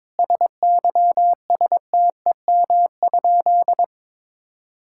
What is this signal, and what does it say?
Morse code, heard as a keyed tone, about 22 wpm, 700 Hz. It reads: SYSTEM?
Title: Sentences from Top 1000 Words - Review Part 6 - 22wpm